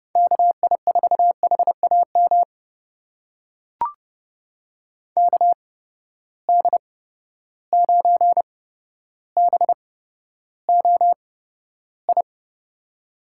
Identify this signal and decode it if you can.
Morse code, heard as a keyed tone, about 30 wpm, spaced as for 13 wpm, 700 Hz. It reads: KI4HAM KD9BOI
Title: USA Call Signs - 6 Characters - 8x Character Spacing - 30wpm